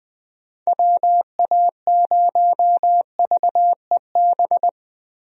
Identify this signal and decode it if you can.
Morse code, heard as a keyed tone, about 20 wpm, 700 Hz. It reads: WA0VEB
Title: USA Call Signs - 6 Characters - 20wpm